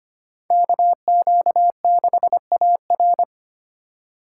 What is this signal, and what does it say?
Morse code, heard as a keyed tone, about 28 wpm, 700 Hz. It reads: KQ6AR
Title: Speed Racing - USA Call Signs 25wpm